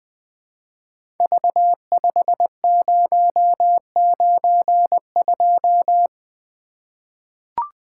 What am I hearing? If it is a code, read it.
Morse code, heard as a keyed tone, about 20 wpm, 700 Hz. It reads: V5092
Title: Code Groups - 5 Letter:Numbers - 20wpm